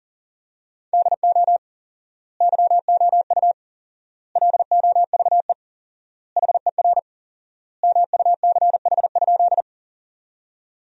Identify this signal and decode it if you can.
Morse code, heard as a keyed tone, about 40 wpm, 700 Hz. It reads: DO YOU LOVE HER MUCH?
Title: 4x Word-Spacing - Sentences from Top 400 Words - 40wpm